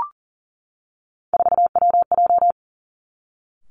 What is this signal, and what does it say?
Morse code, heard as a keyed tone, about 41 wpm, 700 Hz. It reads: 4WJ
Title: Three Challenging Letter-Number Combinations 40wpm